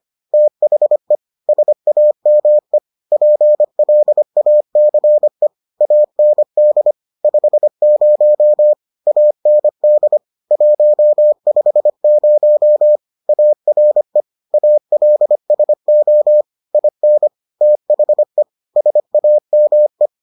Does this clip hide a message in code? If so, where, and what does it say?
Yes: Morse code from 0.3 to 20.1 s, heard as a keyed tone, about 25 wpm, 600 Hz. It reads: THE SAME PLACE AND 50 AND 150 ARE ALSO IN THE SAME